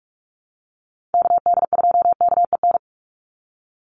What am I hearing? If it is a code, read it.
Morse code, heard as a keyed tone, about 44 wpm, 700 Hz. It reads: KD2KEN
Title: USA Call Signs 45wpm